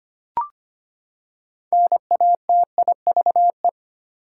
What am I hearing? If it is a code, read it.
Morse code, heard as a keyed tone, about 25 wpm, 700 Hz. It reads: NATIVE
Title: Six Letter Words - 25wpm